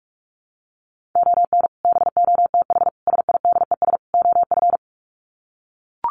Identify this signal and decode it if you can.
Morse code, heard as a keyed tone, about 44 wpm, 700 Hz. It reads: ON BOTH SIDES OF
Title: Sets of 4 Words 45wpm